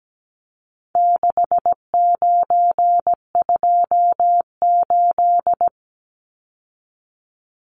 Morse code, heard as a keyed tone, about 17 wpm, 700 Hz. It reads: 6928
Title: Numbers - 4-Digit 17wpm